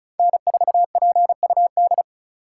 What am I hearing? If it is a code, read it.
Morse code, heard as a keyed tone, about 36 wpm, 700 Hz. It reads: N4PUD
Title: USA Call Signs - 5 Characters - 2x Character Spacing - 35wpm